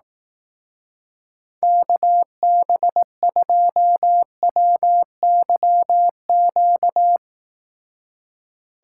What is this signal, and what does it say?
Morse code, heard as a keyed tone, about 18 wpm, 700 Hz. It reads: KB2WYQ
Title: Speed Racing - USA 2x3 Call Signs 15wpm